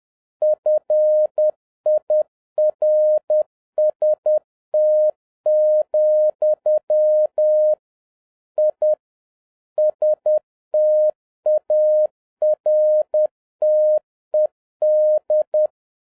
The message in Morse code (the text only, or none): FIRST, I STARTED